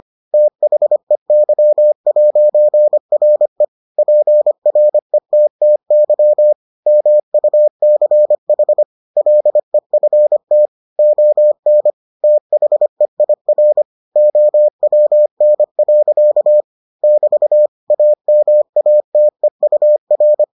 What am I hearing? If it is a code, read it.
Morse code, heard as a keyed tone, about 25 wpm, 600 Hz. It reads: THEY'RE PRETTY MUCH LEFT ON THEIR OWN. <BT> AMATEUR